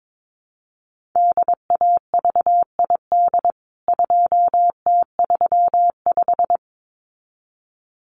Morse code, heard as a keyed tone, about 22 wpm, 700 Hz. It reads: DAVID 2T35
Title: CWT Exchange 22wpm